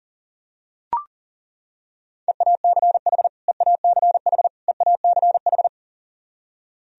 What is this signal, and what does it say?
Morse code, heard as a keyed tone, about 40 wpm, 700 Hz. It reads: EACH EACH EACH
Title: Top 100 Words Repeated 40wpm